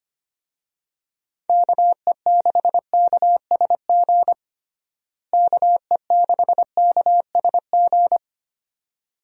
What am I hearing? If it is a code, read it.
Morse code, heard as a keyed tone, about 25 wpm, 700 Hz. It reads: KE6KSG KE6KSG
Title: USA Call Signs - 6 Characters - Repeated 2x - 25wpm